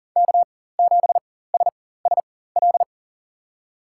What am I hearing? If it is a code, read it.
Morse code, heard as a keyed tone, about 40 wpm, spaced as for 19 wpm, 700 Hz. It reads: K7SSL
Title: USA Call Signs - 4x Character Spacing - 40wpm